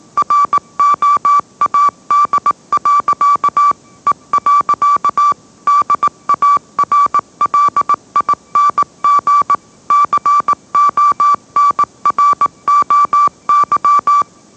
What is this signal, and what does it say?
Morse code, heard as a keyed tone, about 18 wpm, 1160 Hz. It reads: ROAD. E. DARLING CONROY
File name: mrx / mrx082 20wpm